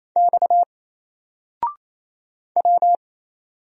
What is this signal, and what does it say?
Morse code, heard as a keyed tone, about 27 wpm, 700 Hz. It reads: X W
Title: Single Character (Letter:Number:Punctuation:Prosign) 28wpm